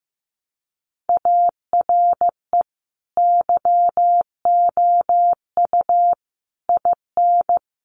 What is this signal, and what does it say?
Morse code, heard as a keyed tone, about 15 wpm, 700 Hz. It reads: ARE YOU IN
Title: Sentences from Top 1300 Words 15wpm